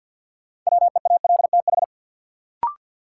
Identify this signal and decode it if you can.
Morse code, heard as a keyed tone, about 47 wpm, 700 Hz. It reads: WEALTH
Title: Sentences from Top 1200 Words - New Word Drill Alt - 50wpm